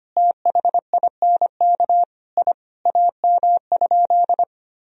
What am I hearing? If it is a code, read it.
Morse code, heard as a keyed tone, about 25 wpm, 700 Hz. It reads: THINK I AM?
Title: Sentences from Top 1000 Words - Review Part 3 - 25wpm